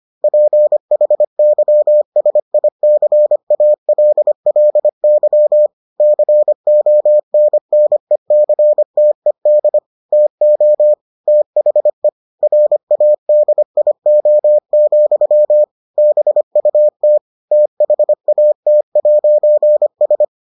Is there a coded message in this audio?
Yes: Morse code from 0.2 to 20.2 s, heard as a keyed tone, about 25 wpm, 600 Hz. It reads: PHYSICALLY CONNECTED TO THE RADIO, BUT THAT'S